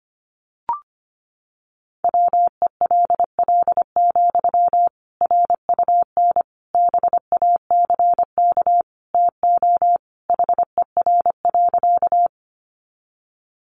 Morse code, heard as a keyed tone, about 25 wpm, 700 Hz. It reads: WELL, RUN BACK TO HER.
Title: Sentences from Top 300 Words 25wpm